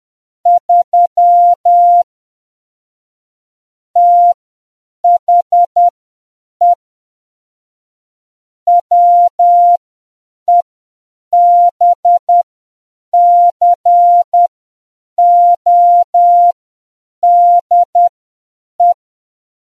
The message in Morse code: 3 THE WEBCODE